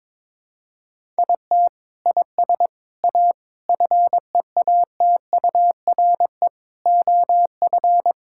Morse code, heard as a keyed tone, about 22 wpm, 700 Hz. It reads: IT IS A FEATURE OF
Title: Sentences from Top 1000 Words - Review Part 5 - 22wpm